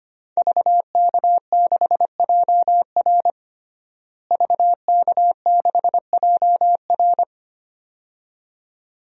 Morse code, heard as a keyed tone, about 25 wpm, 700 Hz. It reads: VK6JR VK6JR
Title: DXCC-100 Call Signs - 5 Characters - Repeated 2x - 25wpm